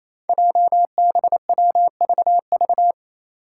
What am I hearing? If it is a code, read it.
Morse code, heard as a keyed tone, about 28 wpm, 700 Hz. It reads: JBWVV
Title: Five Challenging Letter-Number Combinations 28wpm